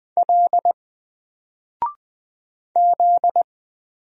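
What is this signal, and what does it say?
Morse code, heard as a keyed tone, about 20 wpm, 700 Hz. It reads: L Z
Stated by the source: Single letter and number Morse drill at 20 wpm, from N4LCD.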